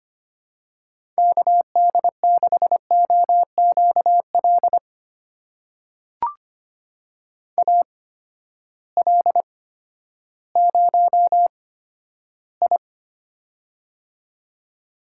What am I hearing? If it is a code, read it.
Morse code, heard as a keyed tone, about 25 wpm, spaced as for 12 wpm, 700 Hz. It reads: KD6OQL AL0I
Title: USA Call Signs - 8x Character Spacing - 25wpm